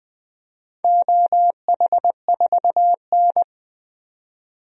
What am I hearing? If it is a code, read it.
Morse code, heard as a keyed tone, about 20 wpm, 700 Hz. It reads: OH4N
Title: DXCC-100 Call Signs - 4 Characters - 20wpm